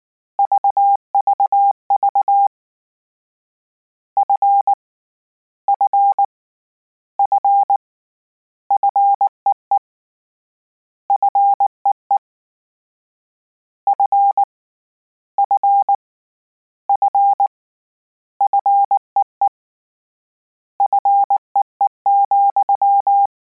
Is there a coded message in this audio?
Yes: Morse code from 0.4 to 23.3 s, heard as a keyed tone, about 19 wpm, 800 Hz. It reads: VVV F F F FEE FEE F F F FEE FEE,